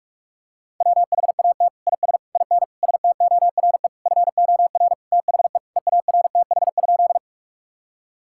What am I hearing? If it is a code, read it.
Morse code, heard as a keyed tone, about 46 wpm, 700 Hz. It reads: WHAT IS IN STORE FOR THE EARTH?
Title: Sentences from Top 1000 Words - Review Part 5 - 45wpm